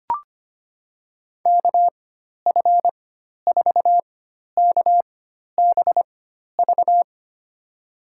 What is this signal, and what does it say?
Morse code, heard as a keyed tone, about 25 wpm, spaced as for 12 wpm, 700 Hz. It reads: KF4KBV
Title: USA Call Signs - 6 Characters - 4x Character Spacing - 25wpm